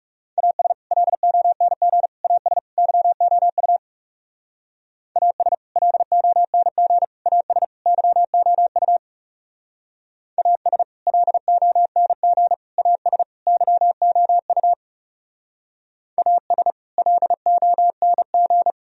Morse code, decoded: AS LONG AS YOU AS LONG AS YOU AS LONG AS YOU AS LONG

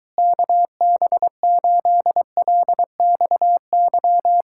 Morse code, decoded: KB8LXY

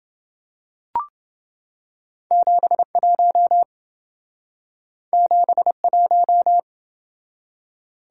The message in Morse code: 71 71

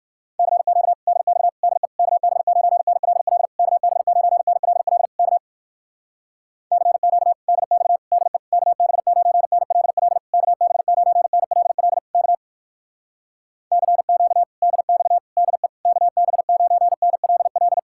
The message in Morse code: CQ DX DE KB9NLL KB9NLL K CQ DX DE KB9NLL KB9NLL K CQ DX DE KB9NLL

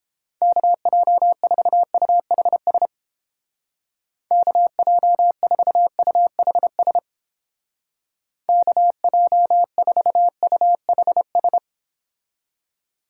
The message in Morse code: KJ4UHS KJ4UHS KJ4UHS

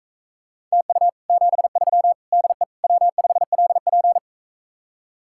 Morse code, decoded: TU 73 DE W5LP

